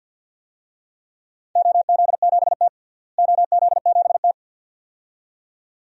Morse code, heard as a keyed tone, about 53 wpm, 700 Hz. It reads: OZ7T OZ7T
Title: DXCC-100 Call Signs - 4 Characters - Repeated 2x - 50wpm